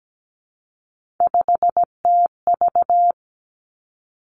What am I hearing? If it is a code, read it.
Morse code, heard as a keyed tone, about 17 wpm, 700 Hz. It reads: 5TV
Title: Three Challenging Letter-Number Combinations 17wpm